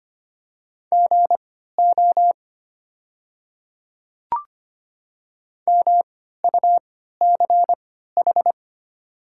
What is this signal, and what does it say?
Morse code, heard as a keyed tone, about 25 wpm, spaced as for 14 wpm, 700 Hz. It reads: GO MUCH
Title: 3x Character Spacing - Top 200 Words - 25wpm